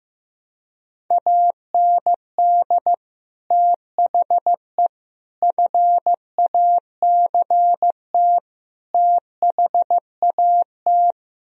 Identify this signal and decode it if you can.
Morse code, heard as a keyed tone, about 15 wpm, 700 Hz. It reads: AND THE FACT THAT
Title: Speed-Racing - Sets of 4 Words 15wpm